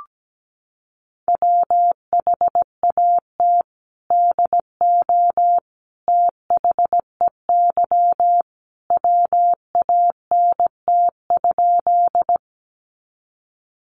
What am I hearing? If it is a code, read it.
Morse code, heard as a keyed tone, about 17 wpm, 700 Hz. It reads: WHAT DO THEY WANT?
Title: Sentences from Top 200 Words 17wpm